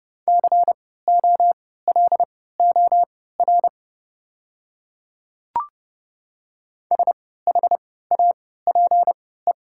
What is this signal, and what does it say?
Morse code, heard as a keyed tone, about 30 wpm, spaced as for 17 wpm, 700 Hz. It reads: COLOR SHAPE
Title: 3x Character Spacing - Top 500 Words - 30wpm